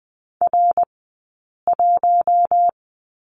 Morse code, heard as a keyed tone, about 20 wpm, 700 Hz. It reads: R 1
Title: 059 Lesson 11d - Letter h - Two Characters 20wpm